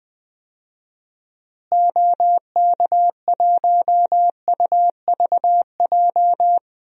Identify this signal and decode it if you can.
Morse code, heard as a keyed tone, about 20 wpm, 700 Hz. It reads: OK1UVJ
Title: DXCC-100 Call Signs - 6 Characters - Repeated 2x - 20wpm